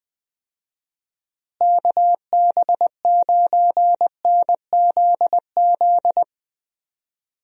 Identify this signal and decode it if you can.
Morse code, heard as a keyed tone, about 20 wpm, 700 Hz. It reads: KB9NZZ
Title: USA Call Signs - 6 Characters - Repeated 2x - 20wpm